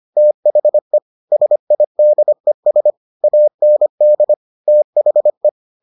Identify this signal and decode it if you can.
Morse code, heard as a keyed tone, about 25 wpm, 600 Hz. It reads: THE SIDES AND THE